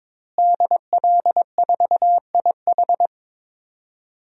Morse code, heard as a keyed tone, about 21 wpm, 700 Hz. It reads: DL4IH
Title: DXCC-100 Call Signs - 5 Characters - Repeated 3x - 22wpm